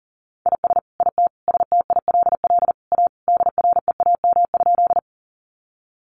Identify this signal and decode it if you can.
Morse code, heard as a keyed tone, about 40 wpm, 700 Hz. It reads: IS IT STILL A DREAM?
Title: Sentences from Top 1500 Words 40wpm